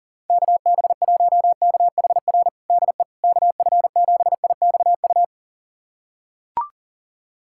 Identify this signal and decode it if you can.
Morse code, heard as a keyed tone, about 40 wpm, 700 Hz. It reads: KB1KHR DE KF7IXU
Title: USA Callsign DE Callsign - 6 Characters - 40wpm